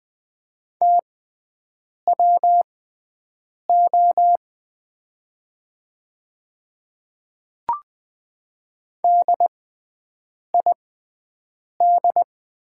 Morse code, heard as a keyed tone, about 20 wpm, spaced as for 7 wpm, 700 Hz. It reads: TWO DID
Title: 6x Character Spacing - Top 100 Words - 20wpm